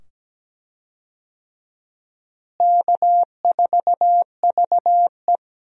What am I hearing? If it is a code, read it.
Morse code, heard as a keyed tone, about 17 wpm, 700 Hz. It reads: K4VE
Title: USA Call Signs - 4 Characters - Repeated 2x - 17wpm